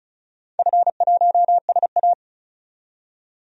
Morse code, heard as a keyed tone, about 34 wpm, 700 Hz. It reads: F1SA
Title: DXCC-100 Call Signs - 4 Characters - 35wpm